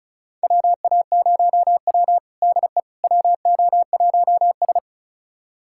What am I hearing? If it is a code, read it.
Morse code, heard as a keyed tone, about 36 wpm, 700 Hz. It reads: WA0W DE WO1S
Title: USA Callsign DE Callsign - 4 Characters - 35wpm